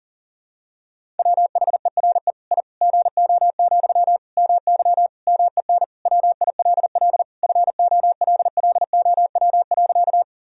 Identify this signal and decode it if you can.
Morse code, heard as a keyed tone, about 40 wpm, 700 Hz. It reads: WHERE I GO, MY MEN WILL FOLLOW.
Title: Sentences from Top 1000 Words - Review Part 1 - 40wpm